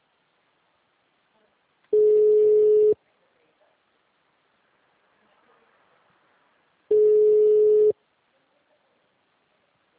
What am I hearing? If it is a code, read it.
Ringback tone (Europe, 425 Hz).